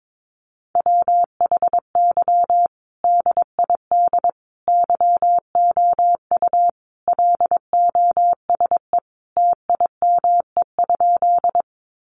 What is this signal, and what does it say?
Morse code, heard as a keyed tone, about 22 wpm, 700 Hz. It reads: WHY DID YOU LOSE TIME?
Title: Sentences from Top 1100 Words 22wpm